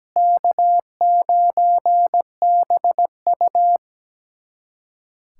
Morse code, heard as a keyed tone, about 17 wpm, 700 Hz. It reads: K9BU
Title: USA Callsign DE Callsign - 4 Characters - 17wpm